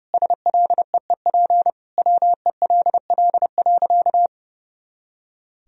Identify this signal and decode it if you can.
Morse code, heard as a keyed tone, about 30 wpm, 700 Hz. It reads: SLEEP WELL.
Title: Sentences from Top 500 Words and Abbreviations 30wpm